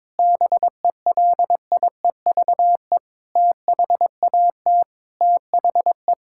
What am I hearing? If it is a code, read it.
Morse code, heard as a keyed tone, about 22 wpm, 700 Hz. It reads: BELIEVE THAT THE
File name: practice / 22wpm / Sentences with 8-Word Phrases - 22wpm